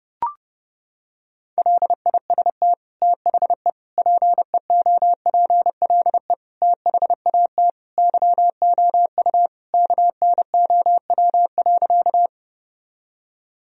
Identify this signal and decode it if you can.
Morse code, heard as a keyed tone, about 32 wpm, 700 Hz. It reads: LIST THE PEOPLE THAT YOU KNOW.
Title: Sentences from Top 1000 Words - Review Part 2 - 30wpm